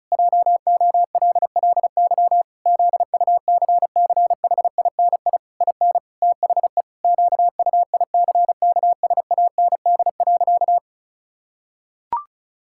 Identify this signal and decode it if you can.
Morse code, heard as a keyed tone, about 34 wpm, 700 Hz. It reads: JOLLY ZUCCHINI IN THE QUICKSAND.
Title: Pangram Challenge v2 - 35wpm